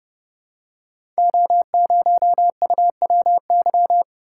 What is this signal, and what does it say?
Morse code, heard as a keyed tone, about 30 wpm, 700 Hz. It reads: O0UWY
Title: Code Groups - 5 Letter:Numbers - 30wpm